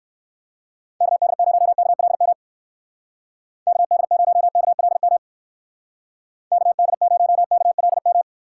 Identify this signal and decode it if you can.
Morse code, heard as a keyed tone, about 60 wpm, 700 Hz. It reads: KD0KLM KD0KLM KD0KLM